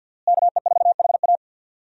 Morse code, heard as a keyed tone, about 52 wpm, 700 Hz. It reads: KE4HA
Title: USA Call Signs - 5 Characters - Repeated 3x - 50wpm